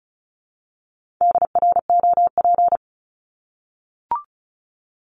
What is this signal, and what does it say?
Morse code, heard as a keyed tone, about 34 wpm, 700 Hz. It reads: DROP
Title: Top 1251-1500 Words 35wpm